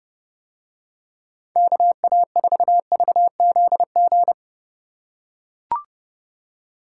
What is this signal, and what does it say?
Morse code, heard as a keyed tone, about 31 wpm, 700 Hz. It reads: KA4VZG